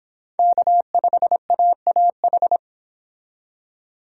Morse code, heard as a keyed tone, about 29 wpm, 700 Hz. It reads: K5AAH